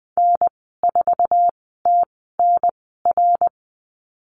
Morse code, heard as a keyed tone, about 20 wpm, spaced as for 15 wpm, 700 Hz. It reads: N4TNR
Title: 088 Lesson 15g - Number 5 - Call Signs 20wpm